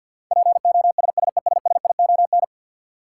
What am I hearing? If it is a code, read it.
Morse code, heard as a keyed tone, about 48 wpm, 700 Hz. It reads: POSSESSION
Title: Sentences from Top 1400 Words - New Word Drill - 50wpm